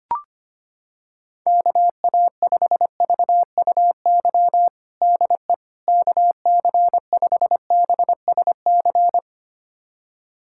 Morse code, heard as a keyed tone, about 25 wpm, 700 Hz. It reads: KA5VUY DE KC5BSC
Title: USA Callsign DE Callsign - 6 Characters - 25wpm